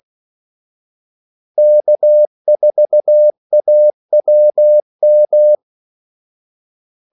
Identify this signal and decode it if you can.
Morse code, heard as a keyed tone, about 16 wpm, 600 Hz. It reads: K4AWM
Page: Kaldesignaler 16 wpm